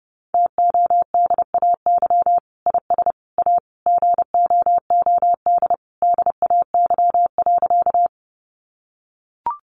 Morse code, heard as a keyed tone, about 30 wpm, 700 Hz. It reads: TODAY IS A GOOD DAY.
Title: Sentences from Top 600 Words 30wpm